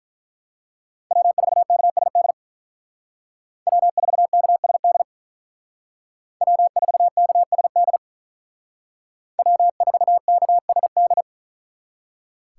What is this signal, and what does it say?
Morse code, heard as a keyed tone, about 44 wpm, 700 Hz. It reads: W4KSD W4KSD W4KSD W4KSD
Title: Speed Racing - USA 1x3 Call Signs 35wpm